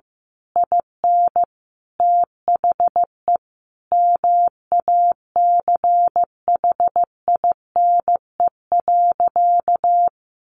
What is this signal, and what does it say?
Morse code, heard as a keyed tone, about 15 wpm, 700 Hz. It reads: IN THE MACHINE.
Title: A Sound Of Thunder 15wpm